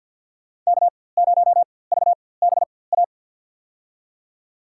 Morse code, heard as a keyed tone, about 50 wpm, spaced as for 23 wpm, 700 Hz. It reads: K0VBA
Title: USA Call Signs - 4x Character Spacing - 50wpm